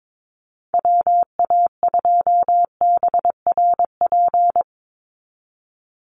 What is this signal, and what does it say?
Morse code, heard as a keyed tone, about 22 wpm, 700 Hz. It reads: WA2BRP
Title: USA Call Signs 22wpm